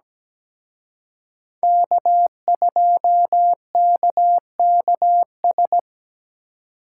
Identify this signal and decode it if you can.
Morse code, heard as a keyed tone, about 20 wpm, 700 Hz. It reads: K2KKS
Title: Speed Racing - USA 1x3 Call Signs 17wpm